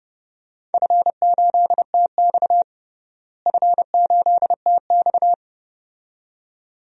Morse code, heard as a keyed tone, about 30 wpm, 700 Hz. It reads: F8TX F8TX
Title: DXCC-100 Call Signs - 4 Characters - Repeated 2x - 30wpm